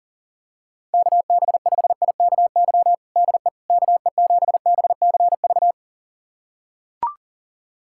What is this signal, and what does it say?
Morse code, heard as a keyed tone, about 40 wpm, 700 Hz. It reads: KB5IKY DE KE7BCV
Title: USA Callsign DE Callsign - 6 Characters - 40wpm